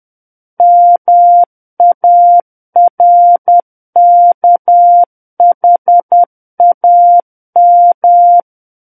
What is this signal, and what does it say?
Morse code, heard as a keyed tone, about 10 wpm, 700 Hz. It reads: MARKHAM
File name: Book One / 01-Chapter 1 - 10 WPM